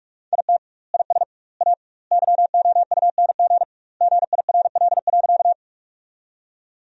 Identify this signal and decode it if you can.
Morse code, heard as a keyed tone, about 41 wpm, 700 Hz. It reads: IT IS A YOUNG GIRL.
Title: 2x Word-Spacing - Sentences from Top 400 Words - 45wpm